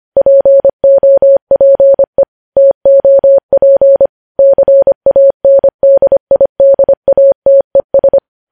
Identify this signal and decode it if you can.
Morse code, heard as a keyed tone, about 25 wpm, 550 Hz. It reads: POPE TOP CANDIDATES
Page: news headlines in morse code at 25 wpm